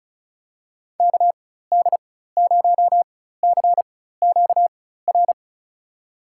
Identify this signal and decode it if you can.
Morse code, heard as a keyed tone, about 34 wpm, spaced as for 16 wpm, 700 Hz. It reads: KD0CQR
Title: USA Call Signs - 6 Characters - 4x Character Spacing - 35wpm